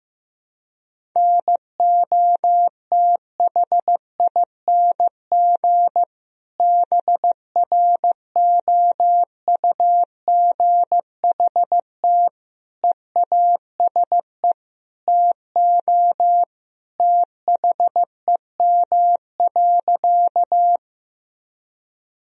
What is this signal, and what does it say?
Morse code, heard as a keyed tone, about 16 wpm, 700 Hz. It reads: NOTHING BROUGHT EASE TO THEM.
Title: Sentences from Top 1000 Words - Review Part 1 - 15wpm